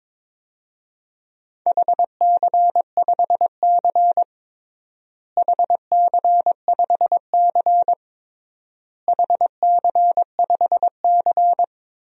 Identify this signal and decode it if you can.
Morse code, heard as a keyed tone, about 22 wpm, 700 Hz. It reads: HC5C HC5C HC5C
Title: DXCC-100 Call Signs - 4 Characters - Repeated 3x - 22wpm